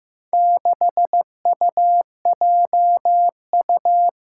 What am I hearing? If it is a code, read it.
Morse code, heard as a keyed tone, about 15 wpm, 700 Hz. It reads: 6UJU
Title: Four Challenging Letter-Number Combinations 15wpm